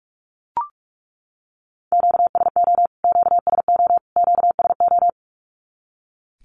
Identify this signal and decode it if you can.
Morse code, heard as a keyed tone, about 44 wpm, 700 Hz. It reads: QSO QSO QSO
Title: Q-Codes Repeated 3x 45wpm